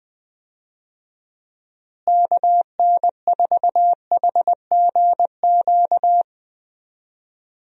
Morse code, heard as a keyed tone, about 21 wpm, 700 Hz. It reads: KN4HGQ
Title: USA Call Signs - 6 Characters - Repeated 3x - 20wpm